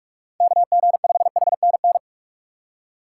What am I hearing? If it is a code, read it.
Morse code, heard as a keyed tone, about 47 wpm, 700 Hz. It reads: KG5HNN
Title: USA Call Signs - 6 Characters - 2x Character Spacing - 45wpm